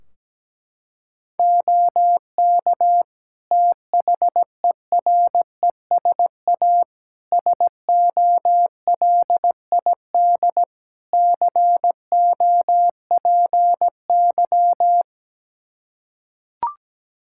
Morse code, heard as a keyed tone, about 17 wpm, 700 Hz. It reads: OK THERESA SOLID COPY
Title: Practice QSO - Exchange 3 - More Info - 17wpm